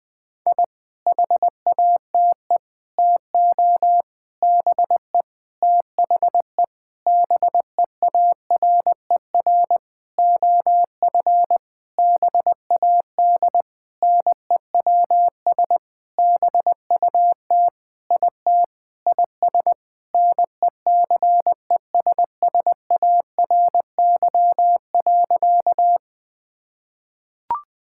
Morse code, heard as a keyed tone, about 20 wpm, 700 Hz. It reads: I HATE TO BE THE BEARER OF BAD NEWS BUT IT IS NECESSARY.
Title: Sentences with 9-Word Phrases - 20wpm